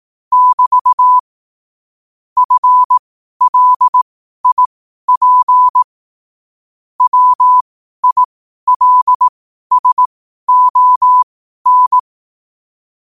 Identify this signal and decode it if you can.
Morse code, heard as a keyed tone, about 18 wpm, spaced as for 12 wpm, 1000 Hz. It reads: <BT> FLIP WILSON
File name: qotdfiles / 12wpm